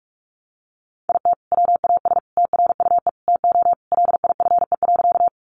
Morse code, heard as a keyed tone, about 44 wpm, 700 Hz. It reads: IT WAS TRUE TO LIFE.
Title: Sentences from Top 400 Words 45wpm